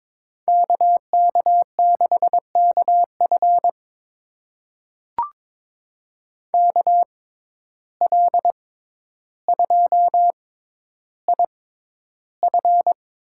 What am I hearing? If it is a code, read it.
Morse code, heard as a keyed tone, about 22 wpm, spaced as for 11 wpm, 700 Hz. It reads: KK6KF KL2IF
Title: USA Call Signs - 5 Characters - 6x Character Spacing - 22wpm